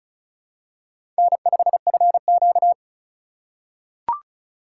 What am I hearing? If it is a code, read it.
Morse code, heard as a keyed tone, about 36 wpm, 700 Hz. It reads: N5FQ
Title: USA Call Signs - 4 Characters - 2x Character Spacing - 35wpm